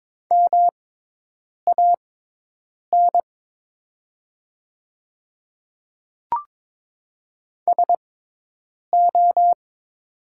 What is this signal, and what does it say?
Morse code, heard as a keyed tone, about 22 wpm, spaced as for 8 wpm, 700 Hz. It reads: MAN SO
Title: 6x Character Spacing - Top 200 Words - 22wpm